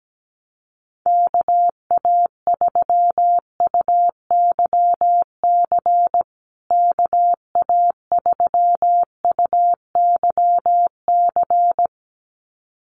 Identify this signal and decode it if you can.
Morse code, heard as a keyed tone, about 17 wpm, 700 Hz. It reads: KA3UYC KA3UYC
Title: USA Call Signs Repeated 2x 17wpm